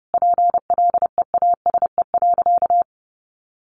Morse code, heard as a keyed tone, about 30 wpm, 700 Hz. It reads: PLEASE.
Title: Sentences from Top 1500 Words 30wpm